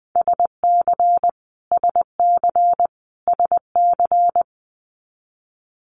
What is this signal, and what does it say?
Morse code, heard as a keyed tone, about 20 wpm, 700 Hz. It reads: SC SC SC
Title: US States Repeated 3x 20wpm